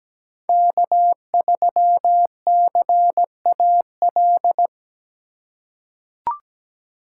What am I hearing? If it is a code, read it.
Morse code, heard as a keyed tone, about 17 wpm, 700 Hz. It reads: K3CAL
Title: USA Call Signs - 5 Characters - 6x Character Spacing - 17wpm